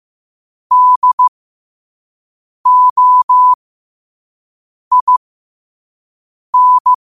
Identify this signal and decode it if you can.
Morse code, heard as a keyed tone, about 15 wpm, spaced as for 5 wpm, 1000 Hz. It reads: DOIN
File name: qotdfiles / 5wpm